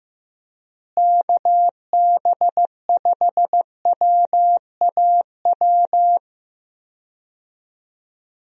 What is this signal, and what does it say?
Morse code, heard as a keyed tone, about 15 wpm, 700 Hz. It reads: KB5WAW
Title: USA Call Signs - 6 Characters - Repeated 3x - 15wpm